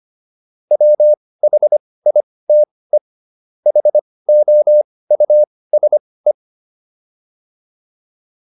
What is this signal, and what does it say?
Morse code, heard as a keyed tone, about 25 wpm, spaced as for 18 wpm, 600 Hz. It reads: WHITE HOUSE